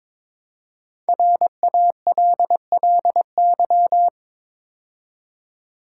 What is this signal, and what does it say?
Morse code, heard as a keyed tone, about 23 wpm, 700 Hz. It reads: RALLY
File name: 5 Letter Words - First 3 Letters Given - 22wpm